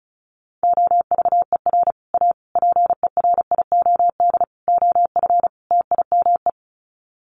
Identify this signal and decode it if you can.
Morse code, heard as a keyed tone, about 34 wpm, 700 Hz. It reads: OVER A PERIOD OF TIME
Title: Sets of 5 Words Encore2 35wpm